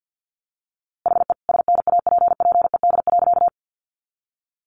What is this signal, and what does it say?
Morse code, heard as a keyed tone, about 50 wpm, 700 Hz. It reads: HE SNAPPED.
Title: A Sound Of Thunder 50wpm